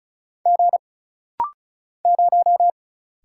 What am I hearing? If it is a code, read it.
Morse code, heard as a keyed tone, about 36 wpm, 700 Hz. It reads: G 0
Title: Single Letter-Number - Rapid-Fire 35wpm